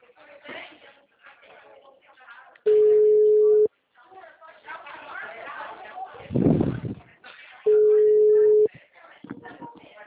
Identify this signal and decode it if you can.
Ringback tone (Europe, 425 Hz), with speech.